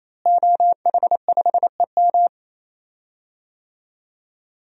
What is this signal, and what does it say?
Morse code, heard as a keyed tone, about 28 wpm, 700 Hz. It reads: OH5EM
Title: DXCC-100 Call Signs - 5 Characters - Repeated 3x - 28wpm